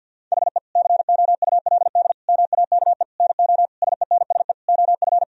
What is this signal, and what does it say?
Morse code, heard as a keyed tone, about 50 wpm, 700 Hz. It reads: HE COULD MAKE NO SENSE OF
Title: Sentences from Top 1000 Words - Review Part 4 - 50wpm